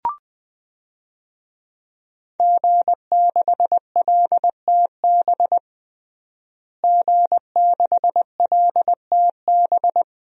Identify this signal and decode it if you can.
Morse code, heard as a keyed tone, about 20 wpm, 700 Hz. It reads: G6LTB G6LTB
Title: DXCC-100 Call Signs - 5 Characters - Repeated 2x - 20wpm